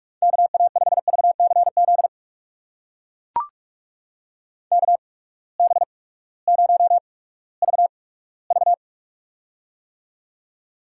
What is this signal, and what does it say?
Morse code, heard as a keyed tone, about 44 wpm, spaced as for 20 wpm, 700 Hz. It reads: KA5VCZ KB0VV
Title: USA Call Signs - 8x Character Spacing - 45wpm